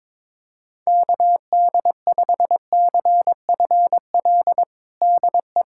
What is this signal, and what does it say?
Morse code, heard as a keyed tone, about 22 wpm, 700 Hz. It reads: KD5CFL DE
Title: USA Callsign DE Callsign 22wpm